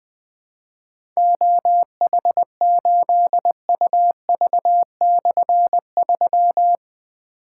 Morse code, heard as a keyed tone, about 21 wpm, 700 Hz. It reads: OH8UV/3
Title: Difficult Call Signs 20wpm